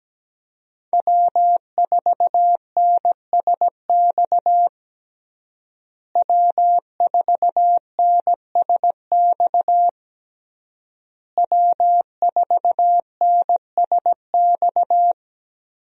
Morse code, heard as a keyed tone, about 17 wpm, 700 Hz. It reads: W4NSX W4NSX W4NSX
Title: USA Call Signs - 5 Characters - Repeated 3x - 17wpm